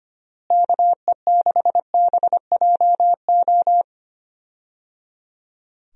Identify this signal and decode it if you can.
Morse code, heard as a keyed tone, about 25 wpm, 700 Hz. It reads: KE6BJO
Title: USA Call Signs - 6 Characters - Repeated 3x - 25wpm